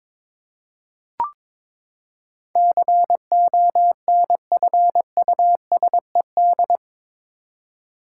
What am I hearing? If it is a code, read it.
Morse code, heard as a keyed tone, about 21 wpm, 700 Hz. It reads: CONFUSED
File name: practice / 22wpm / Eight Letter Words - 22wpm